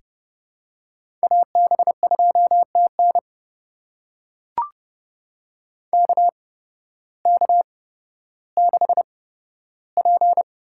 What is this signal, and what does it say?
Morse code, heard as a keyed tone, about 32 wpm, spaced as for 14 wpm, 700 Hz. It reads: AB2TN KK6P
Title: USA Call Signs - 5 Characters - 8x Character Spacing - 30wpm